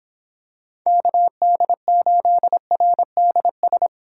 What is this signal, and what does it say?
Morse code, heard as a keyed tone, about 27 wpm, 700 Hz. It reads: KD8RDS
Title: Speed Racing - USA 2x3 Call Signs 22wpm